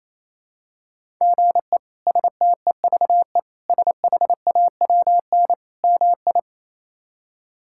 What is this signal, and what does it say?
Morse code, heard as a keyed tone, about 28 wpm, 700 Hz. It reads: GE STEVE SHAWN MI
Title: K1USN SST QSO Practice - 28wpm